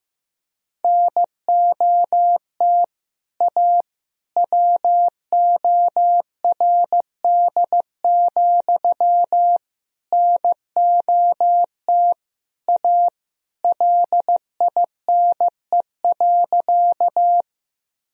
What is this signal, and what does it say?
Morse code, heard as a keyed tone, about 15 wpm, 700 Hz. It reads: NOT A WORD, NOT A LINE.
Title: Sentences from Top 1000 Words - Review Part 1 - 15wpm